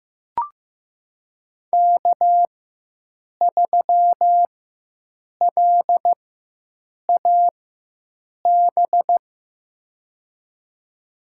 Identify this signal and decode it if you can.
Morse code, heard as a keyed tone, about 15 wpm, spaced as for 7 wpm, 700 Hz. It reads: K3LAB
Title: USA Call Signs - 5 Characters - 4x Character Spacing - 15wpm